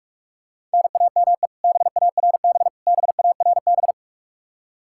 Morse code, heard as a keyed tone, about 42 wpm, 700 Hz. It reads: NAME BARB BARB
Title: Practice QSO - Exchange 1 - Beginning - 45wpm